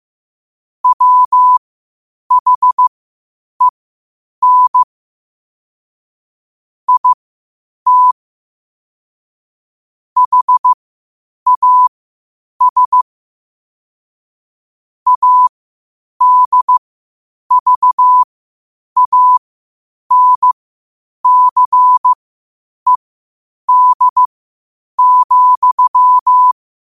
Morse code, heard as a keyed tone, about 15 wpm, spaced as for 8 wpm, 1000 Hz. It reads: WHEN IT HAS ADVANCED,
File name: qotdfiles / 8wpm